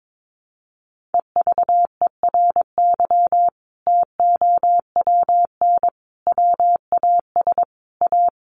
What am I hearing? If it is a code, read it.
Morse code, heard as a keyed tone, about 22 wpm, 700 Hz. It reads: EVERY TOWN WAS A
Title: Sentences from Top 1500 Words 22wpm